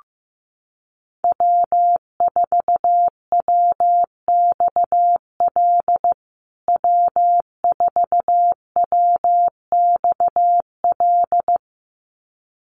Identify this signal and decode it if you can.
Morse code, heard as a keyed tone, about 15 wpm, 700 Hz. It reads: W4WXL W4WXL
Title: USA Call Signs Repeated 2x 15wpm